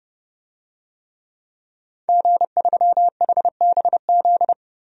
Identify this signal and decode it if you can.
Morse code, heard as a keyed tone, about 31 wpm, 700 Hz. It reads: G3HBZ
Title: DXCC-100 Call Signs - 5 Characters - Repeated 2x - 30wpm